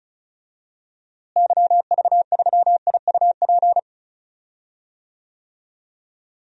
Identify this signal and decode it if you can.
Morse code, heard as a keyed tone, about 34 wpm, 700 Hz. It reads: YV3IUP
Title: DXCC-100 Call Signs - 6 Characters - Repeated 3x - 35wpm